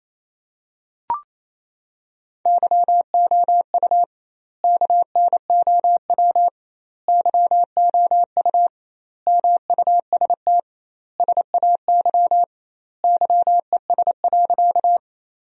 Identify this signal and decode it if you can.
Morse code, heard as a keyed tone, about 28 wpm, 700 Hz. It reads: YOU KNOW YOU MUST SAY YES.
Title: 2x Word-Spacing - Sentences from Top 500 Words - 28wpm